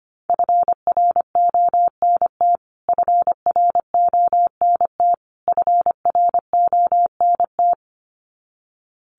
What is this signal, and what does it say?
Morse code, heard as a keyed tone, about 25 wpm, 700 Hz. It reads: FRONT FRONT FRONT
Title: Top 500 Words Repeated 25wpm